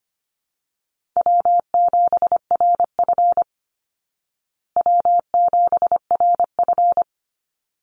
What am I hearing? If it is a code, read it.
Morse code, heard as a keyed tone, about 25 wpm, 700 Hz. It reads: W7RF W7RF
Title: Easy Call Signs Repeated 2x - 25wpm